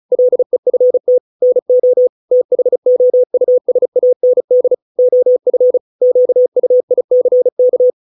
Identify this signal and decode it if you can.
Morse code, heard as a keyed tone, about 34 wpm, 500 Hz. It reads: LEFT NO THOUSAND OF QUICK